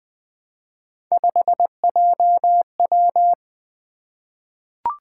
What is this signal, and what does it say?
Morse code, heard as a keyed tone, about 20 wpm, 700 Hz. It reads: 5JW
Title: Three Letter-Number Combinations - 20wpm